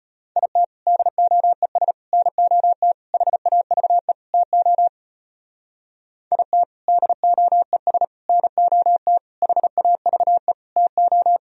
Morse code, heard as a keyed tone, about 32 wpm, 700 Hz. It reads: IT DOES NOT HAVE TO IT DOES NOT HAVE TO